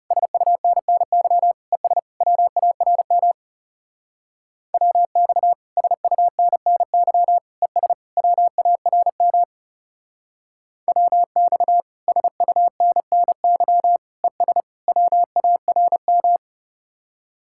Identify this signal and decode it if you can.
Morse code, heard as a keyed tone, about 37 wpm, 700 Hz. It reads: SUNNY ES WARM WX SUNNY ES WARM WX SUNNY ES WARM
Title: Speed-Racing Common QSO Phrases - 30wpm